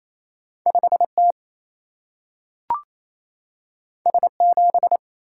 Morse code, heard as a keyed tone, about 28 wpm, 700 Hz. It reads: 5 T S 7
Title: Single-Letter Single-Number 28wpm